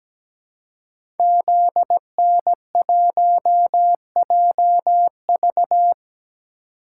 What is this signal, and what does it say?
Morse code, heard as a keyed tone, about 17 wpm, 700 Hz. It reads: ZN1JV
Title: Code Groups - 5 Letter:Numbers - 17wpm